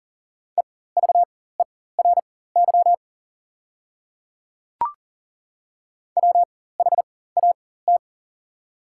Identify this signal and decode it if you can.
Morse code, heard as a keyed tone, about 40 wpm, spaced as for 19 wpm, 690 Hz. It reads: EVERY WHAT
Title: 4x Character Spacing - Top 200 Words - 40wpm